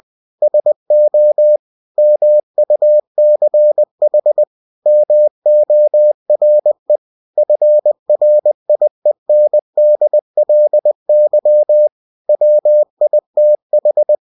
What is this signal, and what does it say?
Morse code, heard as a keyed tone, about 20 wpm, 600 Hz. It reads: SO MUCH MORE FRIENDLY WITH
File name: cw-books / winnie-the-pooh / winnie-20wpm-page-0093